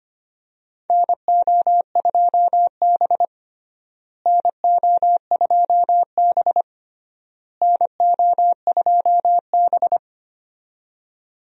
Morse code, heard as a keyed tone, about 25 wpm, 700 Hz. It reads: NO2B NO2B NO2B
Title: USA Call Signs - 4 Characters - Repeated 3x - 25wpm